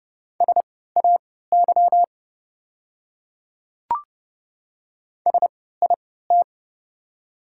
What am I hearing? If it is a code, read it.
Morse code, heard as a keyed tone, about 30 wpm, spaced as for 17 wpm, 700 Hz. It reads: SAY SIT
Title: 3x Character Spacing - Top 500 Words - 30wpm